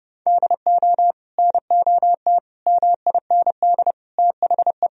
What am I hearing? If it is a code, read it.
Morse code, heard as a keyed tone, about 30 wpm, 700 Hz. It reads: DO NOT MIND THE
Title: Sentences from Top 1000 Words - Review Part 1 - 30wpm